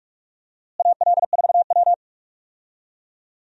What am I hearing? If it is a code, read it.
Morse code, heard as a keyed tone, about 42 wpm, 700 Hz. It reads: AL4W
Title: USA Call Signs - 4 Characters - Repeated 2x - 45wpm